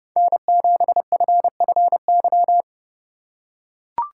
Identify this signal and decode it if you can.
Morse code, heard as a keyed tone, about 30 wpm, 700 Hz. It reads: N7FFY
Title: USA Call Signs - 2x Character Spacing - 30wpm